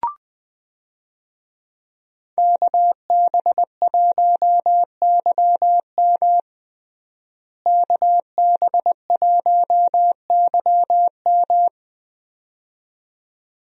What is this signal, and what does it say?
Morse code, heard as a keyed tone, about 20 wpm, 700 Hz. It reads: KB1YM KB1YM
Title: USA Call Signs - 5 Characters - Repeated 2x - 20wpm